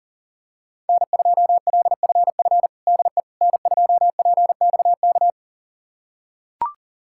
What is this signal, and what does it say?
Morse code, heard as a keyed tone, about 37 wpm, 700 Hz. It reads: N2LFF DE N2PXK
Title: USA Callsign DE Callsign - 5 Characters - 40wpm